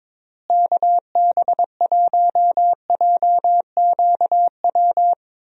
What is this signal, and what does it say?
Morse code, heard as a keyed tone, about 22 wpm, 700 Hz. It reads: KB1JQW
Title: USA Call Signs - 6 Characters - Repeated 3x - 22wpm